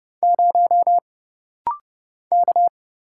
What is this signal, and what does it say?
Morse code, heard as a keyed tone, about 30 wpm, 700 Hz. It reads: 0 K
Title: Single Letter-Number - Rapid-Fire 30wpm